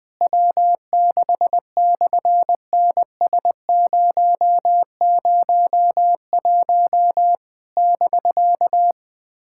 Morse code, heard as a keyed tone, about 20 wpm, 700 Hz. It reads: W6/NS001 <BK>
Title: SOTA QSO Practice 20wpm